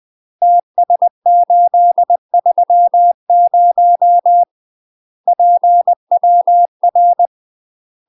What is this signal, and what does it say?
Morse code, heard as a keyed tone, about 20 wpm, 700 Hz. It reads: TS830 PWR
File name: UK MASTER QSOs TRAINING / 20WPM / UK MASTER QSOs 0021